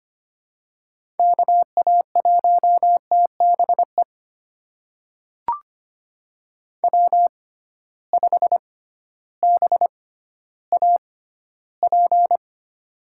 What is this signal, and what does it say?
Morse code, heard as a keyed tone, about 25 wpm, spaced as for 14 wpm, 700 Hz. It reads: KA1TBE W5BAP
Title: USA Call Signs - 6x Character Spacing - 25wpm